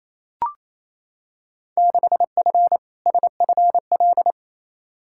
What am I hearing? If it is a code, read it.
Morse code, heard as a keyed tone, about 28 wpm, 700 Hz. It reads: 6F SFL
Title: Field Day Exchanges Encore - 28wpm